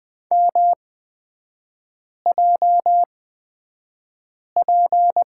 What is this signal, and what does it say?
Morse code, heard as a keyed tone, about 20 wpm, spaced as for 5 wpm, 690 Hz. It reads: MJP
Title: 229 Lesson 37e - Letter x - Three Characters 20wpm